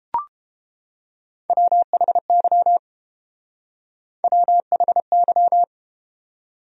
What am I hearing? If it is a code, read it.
Morse code, heard as a keyed tone, about 31 wpm, 700 Hz. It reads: WHY WHY